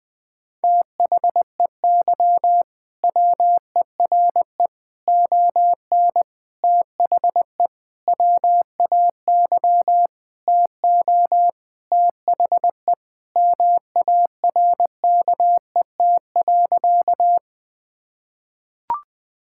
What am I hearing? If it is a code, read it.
Morse code, heard as a keyed tone, about 20 wpm, 700 Hz. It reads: THEY WERE ON THE WAY TO THE MARKET.
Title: Sentences with 5-Word Phrases - Spoken Beginning - 20wpm